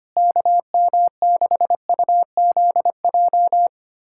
Morse code, heard as a keyed tone, about 25 wpm, 700 Hz. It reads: KM6UZJ
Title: USA Callsign DE Callsign - 6 Characters - 25wpm